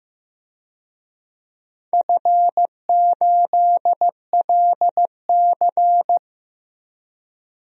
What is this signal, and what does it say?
Morse code, heard as a keyed tone, about 15 wpm, 700 Hz. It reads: F8LC
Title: DXCC-100 Call Signs - 4 Characters - 15wpm